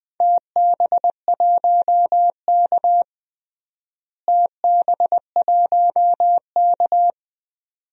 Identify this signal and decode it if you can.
Morse code, heard as a keyed tone, about 21 wpm, 700 Hz. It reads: TB1K TB1K
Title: DXCC-100 Call Signs - 4 Characters - Repeated 2x - 20wpm